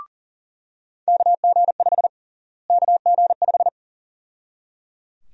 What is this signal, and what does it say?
Morse code, heard as a keyed tone, about 37 wpm, 700 Hz. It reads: KG5 KG5
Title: USA Call Sign Prefixes - Repeated 2x - 40wpm